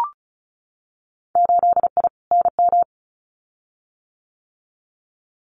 Morse code, heard as a keyed tone, about 36 wpm, 700 Hz. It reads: 8I NM
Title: Winter Field Day Exchanges Encore 35wpm